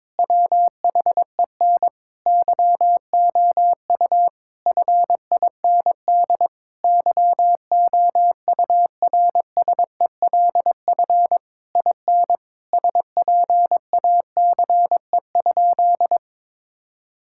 Morse code, heard as a keyed tone, about 21 wpm, 700 Hz. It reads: WHEN YOU FIND YOURSELF IN SPACE?
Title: Sentences from Top 1000 Words - Review Part 5 - 22wpm